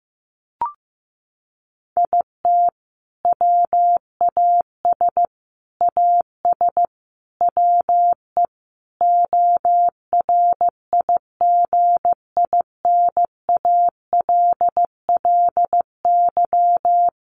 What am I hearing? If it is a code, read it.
Morse code, heard as a keyed tone, about 15 wpm, 700 Hz. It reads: IT WAS AS WE ORIGINALLY
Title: Sentences from Top 1500 Words 15wpm